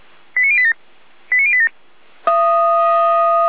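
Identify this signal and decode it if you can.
Five-tone selective call (ZVEI), heard as quick runs of five tones, tones of about 70 ms. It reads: 89087 89087
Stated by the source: ZVEI five-tone call 89090 received by SDR.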